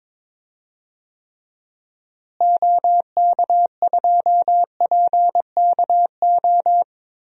Morse code, heard as a keyed tone, about 22 wpm, 700 Hz. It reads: OK2PKO
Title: DXCC-100 Call Signs - 6 Characters - Repeated 2x - 22wpm